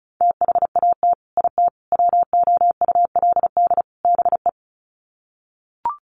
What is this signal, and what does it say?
Morse code, heard as a keyed tone, about 34 wpm, 700 Hz. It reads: THAT IT WOULD BE
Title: Sets of 4 Words 35wpm